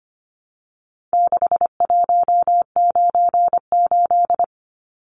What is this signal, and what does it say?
Morse code, heard as a keyed tone, about 25 wpm, 700 Hz. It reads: 6198
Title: Numbers - 4-Digit 25wpm